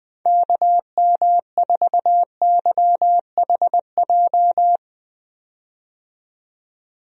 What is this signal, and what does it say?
Morse code, heard as a keyed tone, about 19 wpm, 700 Hz. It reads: KM4YHJ
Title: USA Call Signs - 6 Characters - 20wpm